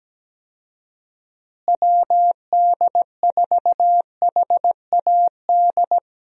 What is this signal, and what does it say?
Morse code, heard as a keyed tone, about 17 wpm, 700 Hz. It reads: WD4HAD
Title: USA Call Signs - 6 Characters - Repeated 3x - 17wpm